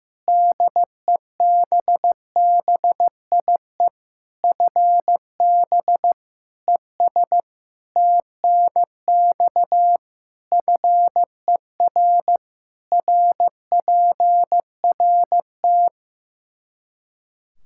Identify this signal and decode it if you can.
Morse code, heard as a keyed tone, about 15 wpm, 700 Hz. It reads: DEBBIE FB ES TNX FER RPRT
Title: Practice QSO - Exchange 2 - Info - 15wpm